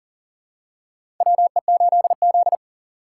Morse code, heard as a keyed tone, about 37 wpm, 700 Hz. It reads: WE8Z
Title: USA Call Signs - 4 Characters - 3x Character Spacing - 40wpm